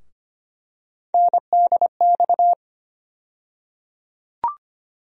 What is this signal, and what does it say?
Morse code, heard as a keyed tone, about 25 wpm, 700 Hz. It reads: NDX
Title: USA Call Sign Postfixes - 3 Letters 25wpm